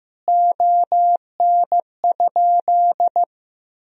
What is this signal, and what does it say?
Morse code, heard as a keyed tone, about 15 wpm, 700 Hz. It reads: ON?
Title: Sentences from Top 1000 Words - Review Part 3 - 15wpm